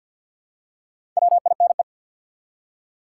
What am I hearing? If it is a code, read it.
Morse code, heard as a keyed tone, about 50 wpm, 700 Hz. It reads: WINE
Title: Four Letter Words - 50wpm